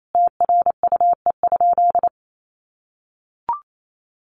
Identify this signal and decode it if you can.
Morse code, heard as a keyed tone, about 28 wpm, 700 Hz. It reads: TRUE?
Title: Sentences from Top 600 Words 28wpm